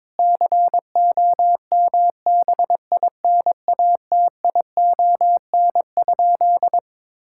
Morse code, heard as a keyed tone, about 22 wpm, 700 Hz. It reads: COMBINATION?
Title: Sentences from Top 1000 Words - Review Part 6 - 22wpm